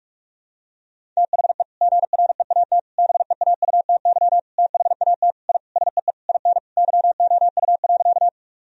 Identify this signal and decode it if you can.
Morse code, heard as a keyed tone, about 42 wpm, 700 Hz. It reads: THE GREAT BEAUTY THAT I SEE IN YOU.
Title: Sentences from Top 1000 Words - Review Part 2 - 45wpm